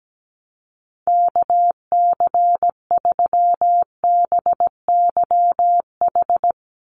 Morse code, heard as a keyed tone, about 17 wpm, 700 Hz. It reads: KC3BYH
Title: USA Call Signs 17wpm